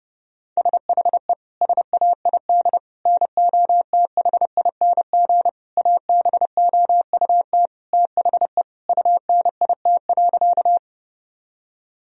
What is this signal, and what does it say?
Morse code, heard as a keyed tone, about 32 wpm, 700 Hz. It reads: SHE SAID NOTHING ABOUT THE UNIT.
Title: Sentences from Top 1000 Words - Review Part 2 - 30wpm